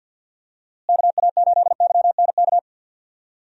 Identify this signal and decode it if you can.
Morse code, heard as a keyed tone, about 52 wpm, 700 Hz. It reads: KA8YNK
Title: USA Call Signs - 6 Characters - Repeated 3x - 50wpm